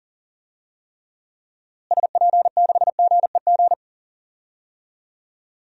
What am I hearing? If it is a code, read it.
Morse code, heard as a keyed tone, about 38 wpm, 700 Hz. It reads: SP6GEG
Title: DXCC-100 Call Signs - 6 Characters - Repeated 3x - 40wpm